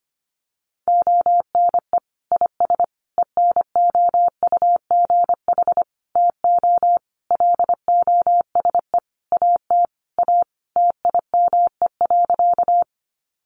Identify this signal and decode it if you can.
Morse code, heard as a keyed tone, about 25 wpm, 700 Hz. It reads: ONE IS ENOUGH TO LOSE AT A TIME.
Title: Sentences from Top 1100 Words 25wpm